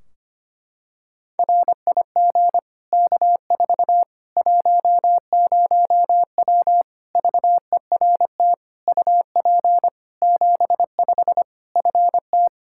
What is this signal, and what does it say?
Morse code, heard as a keyed tone, about 25 wpm, 700 Hz. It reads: RIG K4 10W VERT UP 75 FT
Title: Practice QSO - Exchange 2 - Info - 25wpm